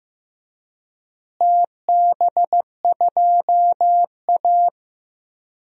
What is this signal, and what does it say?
Morse code, heard as a keyed tone, about 15 wpm, 700 Hz. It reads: TB2A
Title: DXCC-100 Call Signs - 4 Characters - Repeated 3x - 15wpm